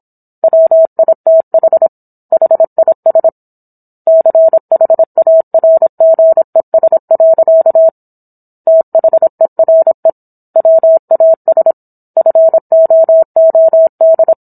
Morse code, heard as a keyed tone, about 27 wpm, 650 Hz. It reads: WITH HIS CHARGES. THERE WAS FOOD